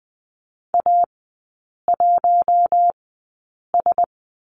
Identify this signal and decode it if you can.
Morse code, heard as a keyed tone, about 20 wpm, spaced as for 8 wpm, 700 Hz. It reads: A1S
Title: 041 Lesson 8e - Number 1 - Three Characters 20wpm